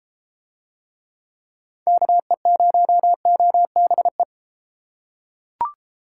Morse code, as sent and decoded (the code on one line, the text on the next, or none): -.- . ----- --- -... .
KE0OBE